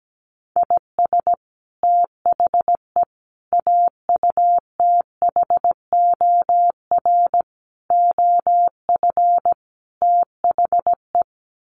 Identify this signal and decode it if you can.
Morse code, heard as a keyed tone, about 17 wpm, 700 Hz. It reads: IS THE AUTHOR OF THE
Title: Sets of 5 Words Encore 17wpm